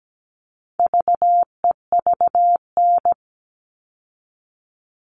Morse code, heard as a keyed tone, about 17 wpm, 700 Hz. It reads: VEVN
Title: All Cut Numbers - 4-Digit 17wpm